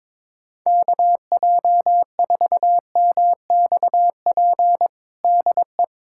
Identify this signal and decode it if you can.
Morse code, heard as a keyed tone, about 22 wpm, 700 Hz. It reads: KJ4MXP DE
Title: USA Callsign DE Callsign 22wpm